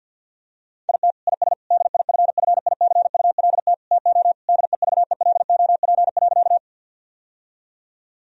Morse code, heard as a keyed tone, about 50 wpm, 700 Hz. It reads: IT IS DIFFICULT TO DEVELOP.
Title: Sentences from Top 1000 Words - Review Part 2 - 50wpm